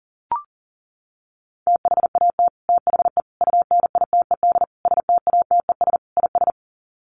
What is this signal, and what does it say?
Morse code, heard as a keyed tone, about 40 wpm, 700 Hz. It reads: THAT THE UNITED STATES IS
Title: Sets of 5 Words Encore 40wpm